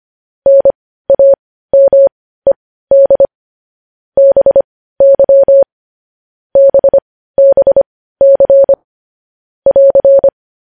Morse code, heard as a keyed tone, about 25 wpm, spaced as for 15 wpm, 550 Hz. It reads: NAMED BY BBC <AR>
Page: News Headlines in Morse Code at 15 WPM